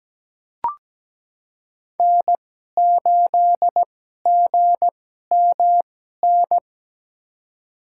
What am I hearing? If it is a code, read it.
Morse code, heard as a keyed tone, about 17 wpm, spaced as for 12 wpm, 700 Hz. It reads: N8GMN
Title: USA Call Signs - 2x Character Spacing - 17wpm